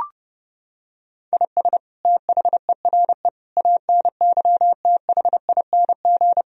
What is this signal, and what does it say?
Morse code, heard as a keyed tone, about 30 wpm, 700 Hz. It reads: IS THERE ANYTHING
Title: Sentences from Top 1000 Words - Review Part 4 - 30wpm